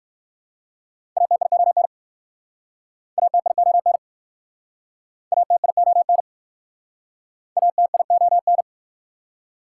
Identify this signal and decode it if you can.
Morse code, heard as a keyed tone, about 60 wpm, 700 Hz. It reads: ATION ATION ATION ATION